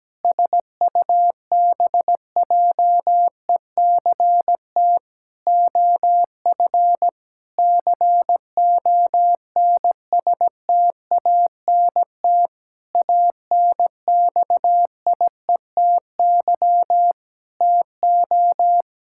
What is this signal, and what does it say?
Morse code, heard as a keyed tone, about 17 wpm, 700 Hz. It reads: SUBJECT OF CONSTANT ANXIETY TO
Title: Sentences from Top 1600 Words 17wpm